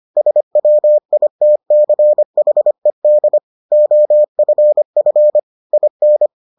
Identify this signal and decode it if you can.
Morse code, heard as a keyed tone, about 25 wpm, 600 Hz. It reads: SWITCHED OFF IN